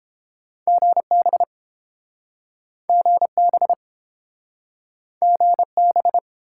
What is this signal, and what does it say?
Morse code, heard as a keyed tone, about 30 wpm, 700 Hz. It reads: GB GB GB